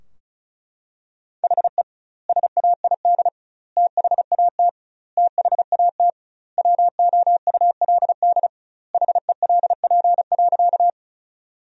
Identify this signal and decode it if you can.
Morse code, heard as a keyed tone, about 36 wpm, 700 Hz. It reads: HE SAID THAT THAT WOULD HELP.
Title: 2x Word-Spacing - Sentences from Top 200 Words - 35wpm